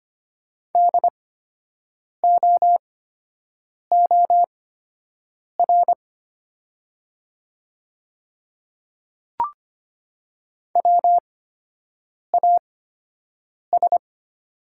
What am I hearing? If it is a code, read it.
Morse code, heard as a keyed tone, about 25 wpm, spaced as for 7 wpm, 700 Hz. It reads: DOOR WAS
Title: 8x Character Spacing - Top 500 Words - 25wpm